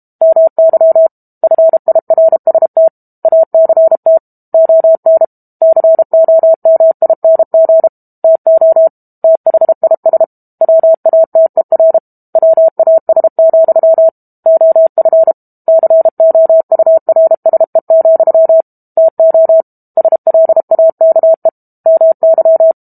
Morse code, heard as a keyed tone, about 32 wpm, 650 Hz. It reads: MY FIRST ACT ON COMING TO THIS WATER WAS, OF COURSE, TO SLAKE MY
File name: Book Two / 06-Chapter 6 - 32 WPM